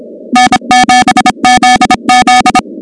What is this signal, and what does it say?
Morse code, heard as a keyed tone, about 26 wpm, 750 Hz, with noise. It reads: N7ZZ
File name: contests / 2019-cw-wpx-kt5j-k5pi- -2330 0-16